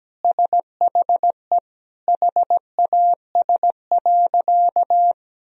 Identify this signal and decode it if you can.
Morse code, heard as a keyed tone, about 17 wpm, 700 Hz. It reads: SHE HAS.
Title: Sentences from Top 1000 Words - Review Part 1 - 17wpm